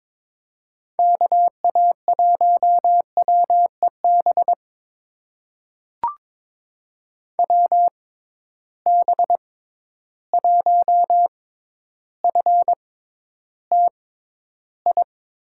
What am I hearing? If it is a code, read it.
Morse code, heard as a keyed tone, about 22 wpm, spaced as for 11 wpm, 700 Hz. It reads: KA1WEB WB1FTI
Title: USA Call Signs - 6 Characters - 6x Character Spacing - 22wpm